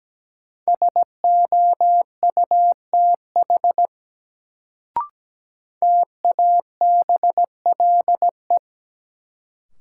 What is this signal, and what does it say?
Morse code, heard as a keyed tone, about 17 wpm, 700 Hz. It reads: SOUTH TABLE
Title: Sentences from Top 400 Words - New Word Drill - 17wpm